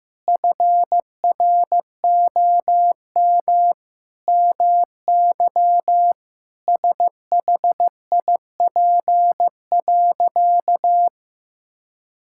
Morse code, heard as a keyed tone, about 15 wpm, 700 Hz. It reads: FROM MY SHIP.
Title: Sentences from Top 1000 Words - Review Part 5 - 15wpm